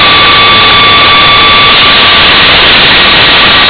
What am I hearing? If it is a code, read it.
PSK31, about 31.25 baud.